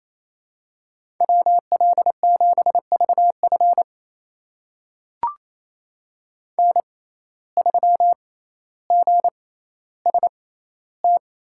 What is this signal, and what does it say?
Morse code, heard as a keyed tone, about 28 wpm, spaced as for 14 wpm, 700 Hz. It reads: WL7VF N3GST
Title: USA Call Signs - 5 Characters - 6x Character Spacing - 28wpm